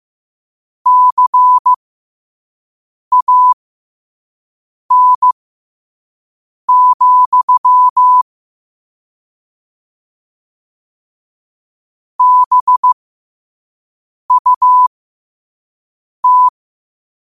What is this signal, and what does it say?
Morse code, heard as a keyed tone, about 15 wpm, spaced as for 5 wpm, 1000 Hz. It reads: CAN, BUT